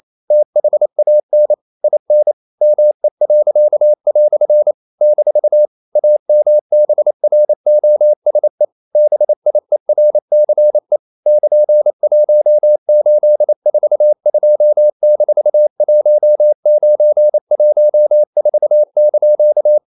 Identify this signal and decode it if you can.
Morse code, heard as a keyed tone, about 28 wpm, 600 Hz. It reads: THAN IN ME." - AMBROSE BIERCE <KN>1842-1914)
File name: CW / Practice / EnglishQuotations0228